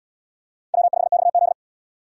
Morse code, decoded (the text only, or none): W5XZ